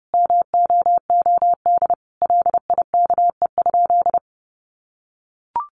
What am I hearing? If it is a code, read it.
Morse code, heard as a keyed tone, about 30 wpm, 700 Hz. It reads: MOOD LIKE?
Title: Sentences from Top 1100 Words 30wpm